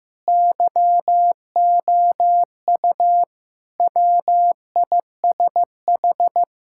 Morse code, heard as a keyed tone, about 15 wpm, 700 Hz. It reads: YOU WISH
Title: Sentences from Top 1600 Words 15wpm